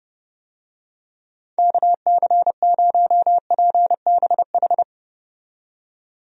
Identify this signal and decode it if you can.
Morse code, heard as a keyed tone, about 30 wpm, 700 Hz. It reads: KC0PBH
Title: USA Call Signs - 6 Characters - 30wpm